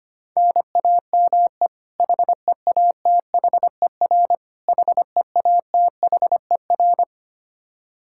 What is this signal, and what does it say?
Morse code, heard as a keyed tone, about 25 wpm, 700 Hz. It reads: NAME HEATHER HEATHER
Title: K1USN SST QSO Practice 25wpm